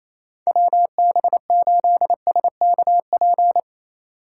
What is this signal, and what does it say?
Morse code, heard as a keyed tone, about 28 wpm, 700 Hz. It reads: WB8SKP
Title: K1USN SST QSO Practice - 28wpm